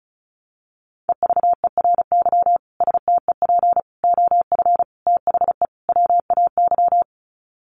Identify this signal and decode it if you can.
Morse code, heard as a keyed tone, about 34 wpm, 700 Hz. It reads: EVERY STEP OF THE WAY
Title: Sets of 5 Words Encore2 35wpm